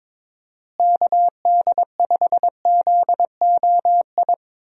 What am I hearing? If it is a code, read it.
Morse code, heard as a keyed tone, about 22 wpm, 700 Hz. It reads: KD5ZOI
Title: USA Call Signs - 6 Characters - Repeated 3x - 22wpm